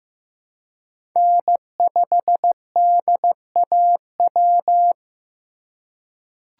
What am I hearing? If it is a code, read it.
Morse code, heard as a keyed tone, about 16 wpm, 700 Hz. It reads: N5DAW
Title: USA Call Signs - 5 Characters - 4x Character Spacing - 15wpm